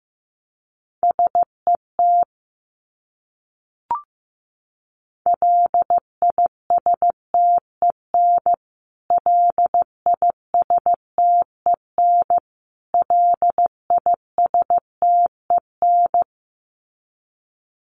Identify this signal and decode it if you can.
Morse code, heard as a keyed tone, about 15 wpm, 700 Hz. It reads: SET LISTEN LISTEN LISTEN
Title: Top 500 Words Repeated 15wpm